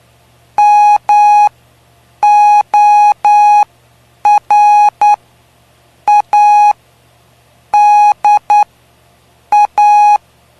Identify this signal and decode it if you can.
Morse code, heard as a keyed tone, about 9 wpm, spaced as for 6 wpm, 830 Hz. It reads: MORADA
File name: cw / py2lcd / 05-PY2LCD-Li��o 5